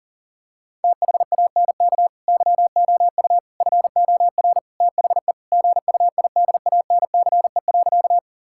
Morse code, heard as a keyed tone, about 40 wpm, 700 Hz. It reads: THANK YOU FOR THE GUIDANCE.
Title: Sentences from Top 1600 Words 40wpm